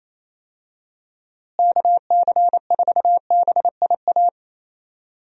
Morse code, heard as a keyed tone, about 27 wpm, 700 Hz. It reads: KC4BIA
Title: USA Call Signs - 6 Characters - Repeated 2x - 28wpm